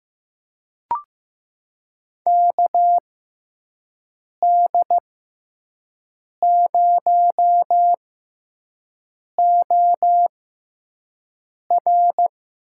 Morse code, heard as a keyed tone, about 15 wpm, spaced as for 5 wpm, 700 Hz. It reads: KD0OR
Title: USA Call Signs - 6x Character Spacing - 15wpm